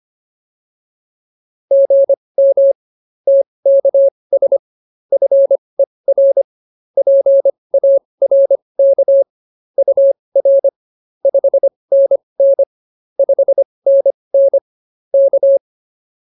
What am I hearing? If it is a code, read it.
Morse code, heard as a keyed tone, about 24 wpm, spaced as for 20 wpm, 550 Hz. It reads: GM TKS FER PARK UR 5NN 5NN K